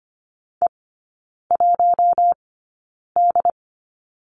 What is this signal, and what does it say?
Morse code, heard as a keyed tone, about 25 wpm, spaced as for 9 wpm, 700 Hz. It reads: E1D
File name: course / 25:10wpm / 093 Lesson 16e - Letter c - Three Characters 25wpm